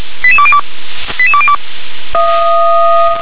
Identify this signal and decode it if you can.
Five-tone selective call (ZVEI), heard as quick runs of five tones, tones of about 70 ms. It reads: 88202 88202